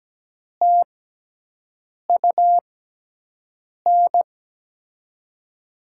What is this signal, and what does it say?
Morse code, heard as a keyed tone, about 17 wpm, spaced as for 6 wpm, 700 Hz. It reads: TUN
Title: USA Call Signs - 6 Characters - 6x Character Spacing - 17wpm